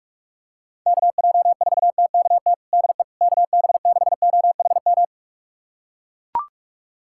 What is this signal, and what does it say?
Morse code, heard as a keyed tone, about 41 wpm, 700 Hz. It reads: KJ4TKT DE KB6OHM
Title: USA Callsign DE Callsign 45wpm